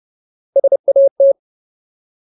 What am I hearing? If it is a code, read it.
Morse code, heard as a keyed tone, about 30 wpm, 550 Hz. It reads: SAT
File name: Ultimate Word List / 30WPM Ultimate